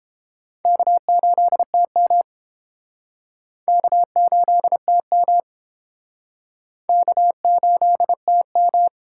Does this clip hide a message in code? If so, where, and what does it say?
Yes: Morse code from 0.7 to 8.9 s, heard as a keyed tone, about 29 wpm, 700 Hz. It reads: K8TM K8TM K8TM